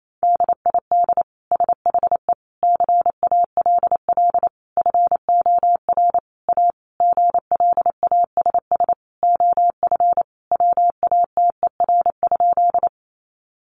Morse code, heard as a keyed tone, about 28 wpm, 700 Hz. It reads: DID SHE CALL FOR A GLASS OF WATER?
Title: Sentences from Top 1300 Words 28wpm